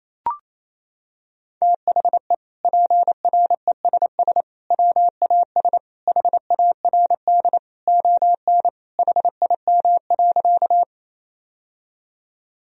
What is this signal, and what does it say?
Morse code, heard as a keyed tone, about 27 wpm, 700 Hz. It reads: THE PRESS WAS HARD ON HIM.
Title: Sentences from Top 1000 Words - Review Part 1 - 28wpm